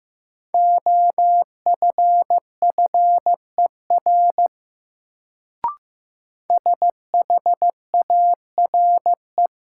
Morse code, heard as a keyed tone, about 15 wpm, 700 Hz. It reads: OFFER SHARE
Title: Sentences from Top 900 Words - New Word Drill - 15wpm